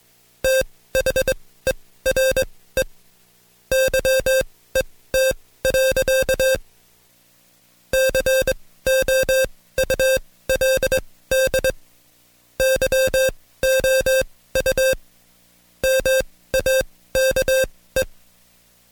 Morse code, heard as a keyed tone, about 22 wpm, spaced as for 16 wpm, 530 Hz. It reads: THERE YET. COULD YOU MAKE
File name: Code Quick / AudioDownld2 / Seg134-Stepper7 15WPM